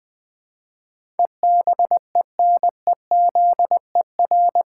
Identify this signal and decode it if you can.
Morse code, heard as a keyed tone, about 20 wpm, 700 Hz. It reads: EBENEZER
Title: Pangram Challenge v2 - 20wpm